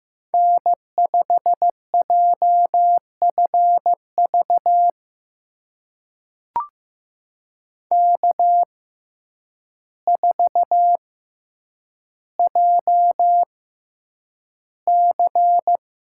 Morse code, heard as a keyed tone, about 15 wpm, spaced as for 8 wpm, 700 Hz. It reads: N5JFV K4JC
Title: USA Call Signs - 5 Characters - 6x Character Spacing - 15wpm